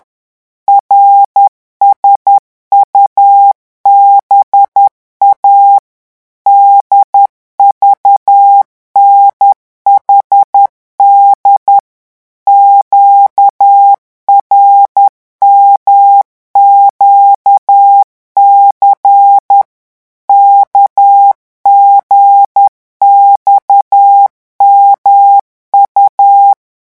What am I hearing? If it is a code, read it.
Morse code, heard as a keyed tone, about 10 wpm, 800 Hz. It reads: RSUBA DVNHD QRMQC KGXMU
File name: morse mp3 / revision5-vitesse 10 mots